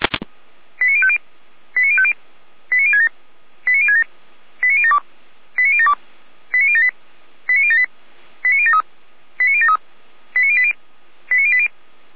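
Five-tone selective call (ZVEI), heard as quick runs of five tones, tones of about 70 ms. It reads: 89050 89050 89067 89067 89072 89072 89078 89078 89073 89073 89080 89080
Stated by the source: ZVEI five-tone selective call 89025 received on SDR.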